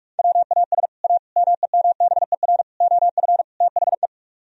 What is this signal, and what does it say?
Morse code, heard as a keyed tone, about 48 wpm, 700 Hz. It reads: WAS A MEMBER OF THE